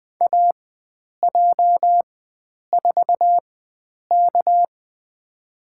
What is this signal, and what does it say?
Morse code, heard as a keyed tone, about 20 wpm, spaced as for 9 wpm, 700 Hz. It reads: AJ4K
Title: USA Call Signs - 4 Characters - 4x Character Spacing - 20wpm